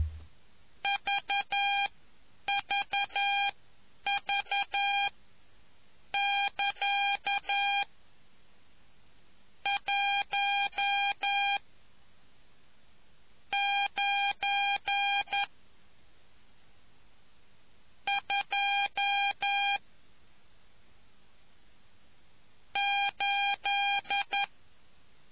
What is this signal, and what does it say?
Morse code, heard as a keyed tone, about 11 wpm, spaced as for 7 wpm, 800 Hz. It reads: VVV<KA> 1 9 2 8